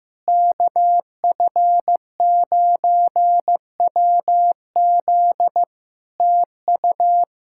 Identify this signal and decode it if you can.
Morse code, heard as a keyed tone, about 15 wpm, 700 Hz. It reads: KF9WZ TU
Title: K1USN SST QSO Practice 15wpm